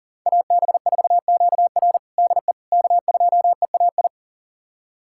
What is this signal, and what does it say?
Morse code, heard as a keyed tone, about 40 wpm, 700 Hz. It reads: AB4QR DE K2EAI
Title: USA Callsign DE Callsign - 5 Characters - 40wpm